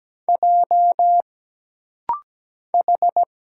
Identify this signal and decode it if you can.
Morse code, heard as a keyed tone, about 17 wpm, 700 Hz. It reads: J H
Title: Single Letter-Number - Rapid-Fire 17wpm